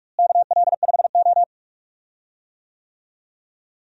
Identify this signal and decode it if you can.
Morse code, heard as a keyed tone, about 42 wpm, 700 Hz. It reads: KL5O
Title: USA Call Signs - 4 Characters - Repeated 3x - 45wpm